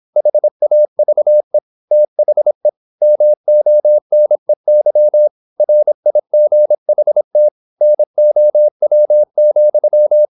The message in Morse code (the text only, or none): HAVE THE MONEY RIGHT NOW,